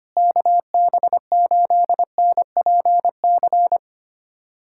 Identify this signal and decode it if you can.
Morse code, heard as a keyed tone, about 25 wpm, 700 Hz. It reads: KB8NPC
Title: USA Callsign DE Callsign - 6 Characters - 25wpm